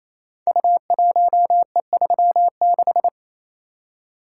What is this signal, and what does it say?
Morse code, heard as a keyed tone, about 28 wpm, 700 Hz. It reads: U1E36
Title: Code Groups - 5 Letter:Numbers - 28wpm